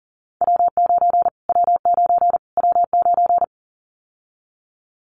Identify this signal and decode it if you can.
Morse code, heard as a keyed tone, about 40 wpm, 700 Hz. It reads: W9 W9 W9
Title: Callsign Prefixes for Top 25 DXCC - Repeated 3x - 40wpm